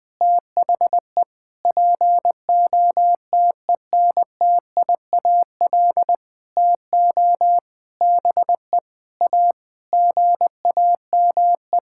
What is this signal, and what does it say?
Morse code, heard as a keyed tone, about 19 wpm, 700 Hz. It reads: THE POTENTIAL TO BE A GAME
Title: Sentences with 6-Word Phrases - 20wpm